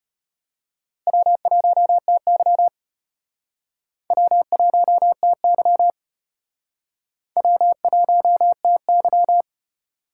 Morse code, heard as a keyed tone, about 34 wpm, 700 Hz. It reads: W1TY W1TY W1TY